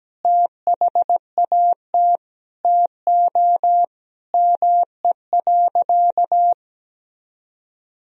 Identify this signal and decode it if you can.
Morse code, heard as a keyed tone, about 17 wpm, 700 Hz. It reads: THAT TO ME.
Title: Sentences from Top 400 Words and Abbreviations 17wpm